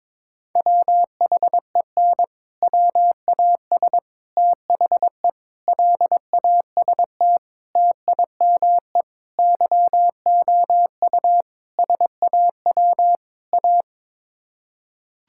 Morse code, heard as a keyed tone, about 22 wpm, 700 Hz. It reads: WHEN WAS THE LAST TIME YOU SAW A
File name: practice / 22wpm / Sets of 8 Words - 22wpm